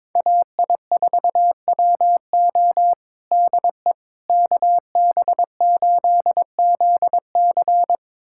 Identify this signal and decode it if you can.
Morse code, heard as a keyed tone, about 22 wpm, 700 Hz. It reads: AI4WO DE KB8ZC
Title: USA Callsign DE Callsign - 5 Characters - 22wpm